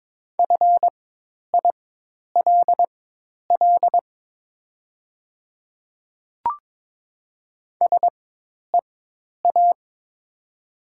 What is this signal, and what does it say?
Morse code, heard as a keyed tone, about 22 wpm, spaced as for 10 wpm, 700 Hz. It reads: FILL SEA
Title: 4x Character Spacing - Top 500 Words - 22wpm